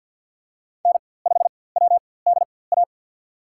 Morse code, heard as a keyed tone, about 52 wpm, spaced as for 23 wpm, 700 Hz. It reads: N5WDA
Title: USA Call Signs - 5 Characters - 4x Character Spacing - 50wpm